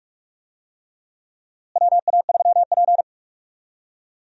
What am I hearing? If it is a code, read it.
Morse code, heard as a keyed tone, about 44 wpm, 700 Hz. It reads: WA3P
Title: USA Call Signs - 4 Characters - Repeated 3x - 45wpm